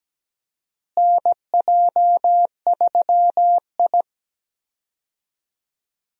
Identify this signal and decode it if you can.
Morse code, heard as a keyed tone, about 17 wpm, 700 Hz. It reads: NJ3I
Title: USA Call Signs - 4 Characters - Repeated 2x - 17wpm